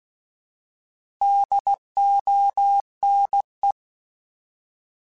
Morse code, heard as a keyed tone, about 16 wpm, 800 Hz. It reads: DONE
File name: Alphabetical / Alphabetical mp3s / Ds @15